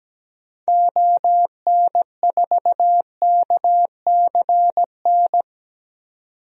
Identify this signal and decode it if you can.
Morse code, heard as a keyed tone, about 17 wpm, 700 Hz. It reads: ON4KCN
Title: DXCC-100 Call Signs - 6 Characters - 17wpm